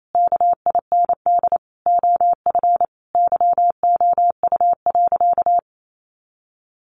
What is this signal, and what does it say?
Morse code, heard as a keyed tone, about 28 wpm, 700 Hz. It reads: KIND OF YOU.
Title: Sentences from Top 200 Words 28wpm